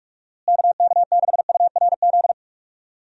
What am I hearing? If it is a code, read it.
Morse code, heard as a keyed tone, about 41 wpm, 700 Hz. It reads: KK6URZ
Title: USA Call Signs - 6 Characters - Repeated 3x - 45wpm